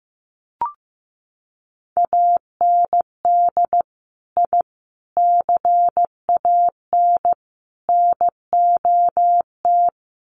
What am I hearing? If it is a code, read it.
Morse code, heard as a keyed tone, about 15 wpm, 700 Hz. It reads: AND I CAN NOT
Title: Sets of 4 Words 15wpm